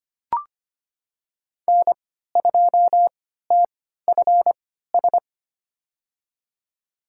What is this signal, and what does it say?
Morse code, heard as a keyed tone, about 25 wpm, spaced as for 14 wpm, 700 Hz. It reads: N2TFS
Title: USA Call Signs - 5 Characters - 3x Character Spacing - 25wpm